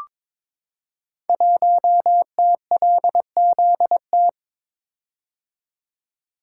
Morse code, heard as a keyed tone, about 22 wpm, 700 Hz. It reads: 1TLZT
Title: Code Groups - 5 Letter:Numbers - 22wpm